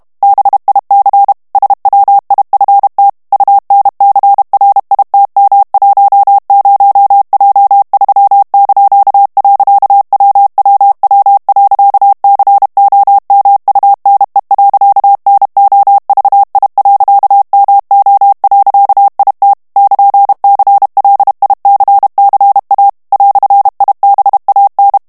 Morse code, heard as a keyed tone, about 32 wpm, 800 Hz. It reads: BIC SWIFT UNCRITM10J3). WWW.COMUNE.NOVI.MO.IT <KN>CLICCA "IBAN
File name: morse mp3 / ITALIEN texte clair 1 vitesse 30 mots minute